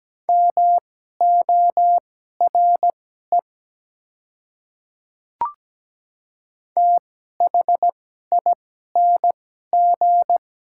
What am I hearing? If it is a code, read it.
Morse code, heard as a keyed tone, about 17 wpm, spaced as for 12 wpm, 700 Hz. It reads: MORE THING
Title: 2x Character Spacing - Top 200 Words - 17wpm